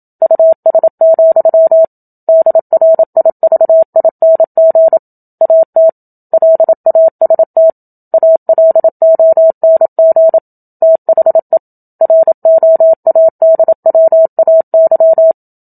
Morse code, heard as a keyed tone, about 27 wpm, 650 Hz. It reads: US, DRIVING AT LAST ALONG THE ROADWAY
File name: Book Two / 01-Chapter 1 - 27 WPM